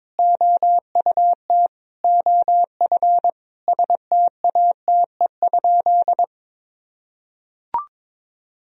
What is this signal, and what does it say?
Morse code, heard as a keyed tone, about 22 wpm, 700 Hz. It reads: OUT OF STATE?
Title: Sentences from Top 1000 Words - Review Part 2 - 22wpm